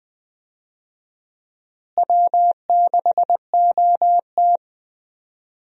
Morse code, heard as a keyed tone, about 20 wpm, 700 Hz. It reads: W6OT
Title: USA Call Signs - 4 Characters - Repeated 3x - 20wpm